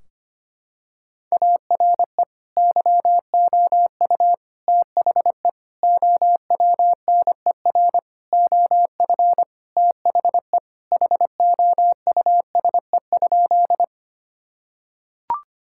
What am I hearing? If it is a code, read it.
Morse code, heard as a keyed tone, about 25 wpm, 700 Hz. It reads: ARE YOU THE OWNER OF THE HOUSE?
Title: Sentences from Top 1600 Words 25wpm